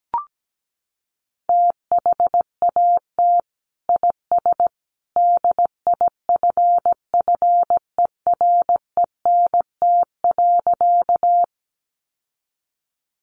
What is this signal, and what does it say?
Morse code, heard as a keyed tone, about 17 wpm, 700 Hz. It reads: THAT IS DIFFERENT.
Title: Sentences from Top 600 Words 17wpm